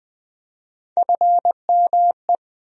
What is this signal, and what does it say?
Morse code, heard as a keyed tone, about 20 wpm, 700 Hz. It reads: FME
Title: USA Call Sign Postfixes - 3 Letters 20wpm